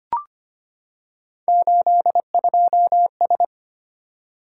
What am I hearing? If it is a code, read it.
Morse code, heard as a keyed tone, about 25 wpm, 700 Hz. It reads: 82S
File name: Three Letter-Number Combinations - 25wpm